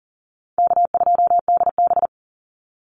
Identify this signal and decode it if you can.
Morse code, heard as a keyed tone, about 40 wpm, 700 Hz. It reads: K2DB
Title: Easy Call Signs - 40wpm